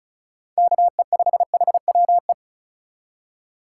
Morse code, heard as a keyed tone, about 36 wpm, 700 Hz. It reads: KE5HWE
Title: USA Call Signs - 6 Characters - 4x Character Spacing - 35wpm